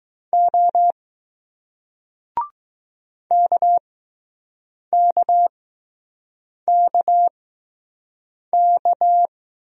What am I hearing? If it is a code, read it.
Morse code, heard as a keyed tone, about 20 wpm, spaced as for 6 wpm, 700 Hz. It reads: O KKKK